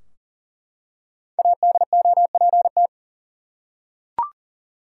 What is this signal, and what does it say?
Morse code, heard as a keyed tone, about 40 wpm, 700 Hz. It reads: ADOPT
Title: Five Letter Words - 40wpm